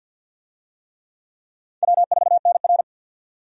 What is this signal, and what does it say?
Morse code, heard as a keyed tone, about 47 wpm, 700 Hz. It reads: W4NR
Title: USA Call Signs - 4 Characters - 50wpm